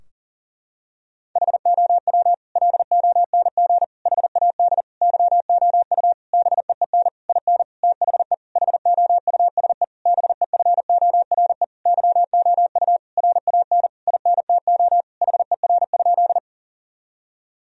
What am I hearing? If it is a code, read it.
Morse code, heard as a keyed tone, about 40 wpm, 700 Hz. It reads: HOW LONG HAD YOU BEEN IN THE HOUSE BEFORE YOU RAN INTO HER?
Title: Sentences from Top 1000 Words - Review Part 3 - 40wpm